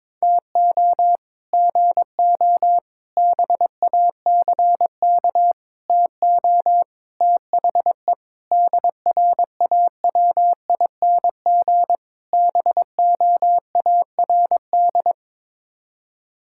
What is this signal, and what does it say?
Morse code, heard as a keyed tone, about 21 wpm, 700 Hz. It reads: TO GO BACK TO THE DRAWING BOARD
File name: practice / 22wpm / Sets of 7 Words - 22wpm